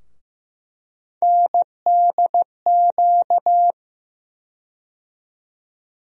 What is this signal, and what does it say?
Morse code, heard as a keyed tone, about 15 wpm, 700 Hz. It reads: NDQ
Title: USA Call Sign Postfixes - 3 Letters 15wpm